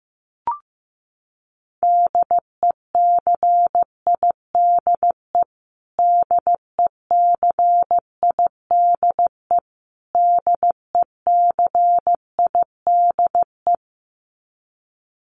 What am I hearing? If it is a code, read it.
Morse code, heard as a keyed tone, about 14 wpm, 700 Hz. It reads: DECIDE DECIDE DECIDE
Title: Top 500 Words Repeated 15wpm